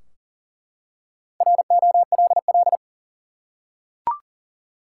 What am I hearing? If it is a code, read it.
Morse code, heard as a keyed tone, about 40 wpm, 700 Hz. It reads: ROLL
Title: Four Letter Words - 40wpm